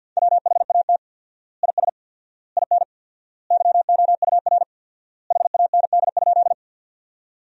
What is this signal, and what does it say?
Morse code, heard as a keyed tone, about 50 wpm, 700 Hz. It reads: WHAT IS IN YOUR HAND?
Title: 4x Word-Spacing - Sentences from Top 200 Words - 50wpm